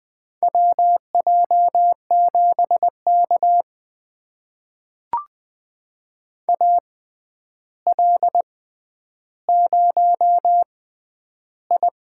Morse code, heard as a keyed tone, about 20 wpm, spaced as for 10 wpm, 700 Hz. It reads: WJ7K AL0I
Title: USA Call Signs - 4 Characters - 6x Character Spacing - 20wpm